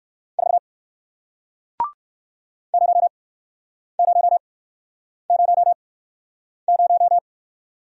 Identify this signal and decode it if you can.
Morse code, heard as a keyed tone, about 60 wpm, spaced as for 9 wpm, 700 Hz. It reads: 4 0000